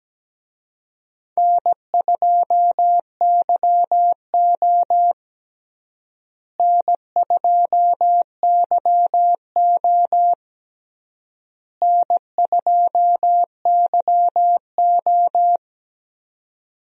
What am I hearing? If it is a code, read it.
Morse code, heard as a keyed tone, about 17 wpm, 700 Hz. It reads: N2YO N2YO N2YO
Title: USA Call Signs - 4 Characters - Repeated 3x - 17wpm